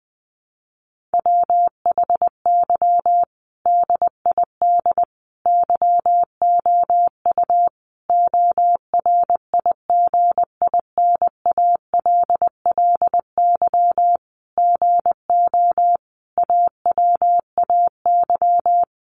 Morse code, heard as a keyed tone, about 20 wpm, 700 Hz. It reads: WHY DID YOU ORIGINALLY GO AWAY
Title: Sentences from Top 1500 Words 20wpm